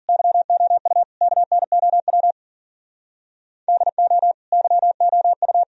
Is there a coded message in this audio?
Yes: Morse code from 0.1 to 5.6 s, heard as a keyed tone, about 55 wpm, 700 Hz. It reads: YOU KNOW DO YOU